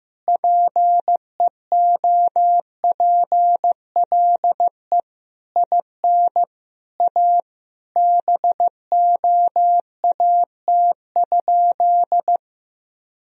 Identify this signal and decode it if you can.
Morse code, heard as a keyed tone, about 15 wpm, 700 Hz. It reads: PEOPLE IN A BOAT?
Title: Sentences from Top 1000 Words - Review Part 3 - 15wpm